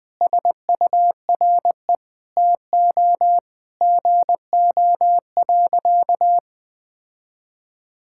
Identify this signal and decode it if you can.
Morse code, heard as a keyed tone, about 20 wpm, 700 Hz. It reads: SURE TO GO.
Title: Sentences from Top 1600 Words 20wpm